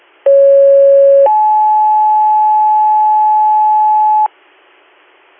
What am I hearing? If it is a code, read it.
Two-tone page, heard as a one-second tone, then a three-second tone, about 550 then 860 Hz, with noise. It is heard from 0.3 to 4.3 s.